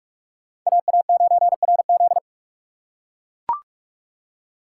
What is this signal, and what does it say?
Morse code, heard as a keyed tone, about 42 wpm, 700 Hz. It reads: AA9RZ